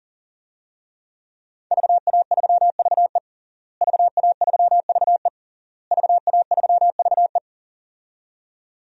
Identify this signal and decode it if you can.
Morse code, heard as a keyed tone, about 40 wpm, 700 Hz. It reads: VA3VE VA3VE VA3VE
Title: DXCC-100 Call Signs - 5 Characters - Repeated 3x - 40wpm